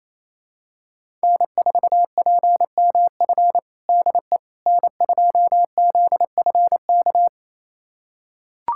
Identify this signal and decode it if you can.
Morse code, heard as a keyed tone, about 27 wpm, 690 Hz. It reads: N4PMF DE N2ZFK
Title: USA Callsign DE Callsign - 5 Characters - 28wpm